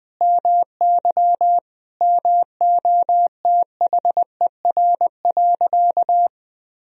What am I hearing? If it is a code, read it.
Morse code, heard as a keyed tone, about 21 wpm, 700 Hz. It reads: MY MOTHER.
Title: Sentences from Top 1000 Words - Review Part 4 - 20wpm